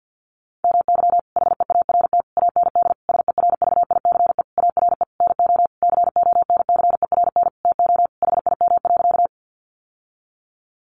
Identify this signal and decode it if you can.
Morse code, heard as a keyed tone, about 50 wpm, 700 Hz. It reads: MY HEART AND SERVICE ARE NO CONCERN TO HIM.
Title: Sentences from Top 1200 Words 50wpm